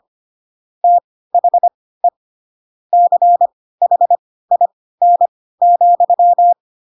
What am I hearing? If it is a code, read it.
Morse code, heard as a keyed tone, about 25 wpm, spaced as for 16 wpm, 700 Hz. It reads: THE CHIN,
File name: Pinocchio / Pinocchio ch3 10 25 ik1hge